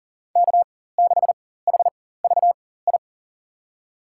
Morse code, heard as a keyed tone, about 39 wpm, spaced as for 18 wpm, 700 Hz. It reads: K6HVI
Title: USA Call Signs - 5 Characters - 4x Character Spacing - 40wpm